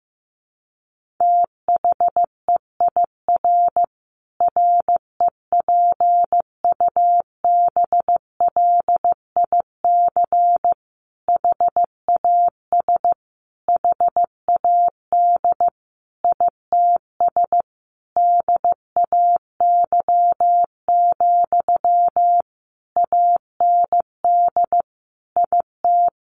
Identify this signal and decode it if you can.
Morse code, heard as a keyed tone, about 15 wpm, 700 Hz. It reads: THEIR REPUBLIC HAS HAD ITS DAY, AND IT
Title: Sentences from Top 1500 Words 15wpm